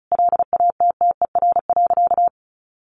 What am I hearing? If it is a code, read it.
Morse code, heard as a keyed tone, about 34 wpm, 700 Hz. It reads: LATTER.
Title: A Sound Of Thunder 35wpm